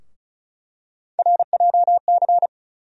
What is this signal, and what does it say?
Morse code, heard as a keyed tone, about 34 wpm, 700 Hz. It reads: RJC
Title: USA Call Sign Postfixes - 3 Letters 35wpm